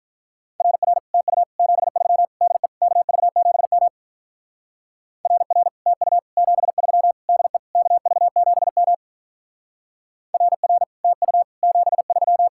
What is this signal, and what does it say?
Morse code, heard as a keyed tone, about 44 wpm, 700 Hz. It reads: RR TU 73 DE KV7M RR TU 73 DE KV7M RR TU 73
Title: Speed-Racing Common QSO Phrases Encore - 35wpm